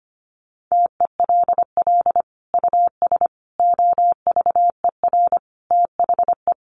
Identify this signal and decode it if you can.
Morse code, heard as a keyed tone, about 25 wpm, 700 Hz. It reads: TELL US OVER THE
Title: Sentences from Top 1100 Words 25wpm